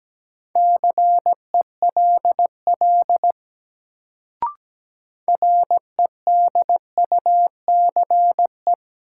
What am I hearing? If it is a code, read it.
Morse code, heard as a keyed tone, about 17 wpm, 700 Hz. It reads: CELL REDUCE
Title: Sentences from Top 1100 Words - New Word Drill - 17wpm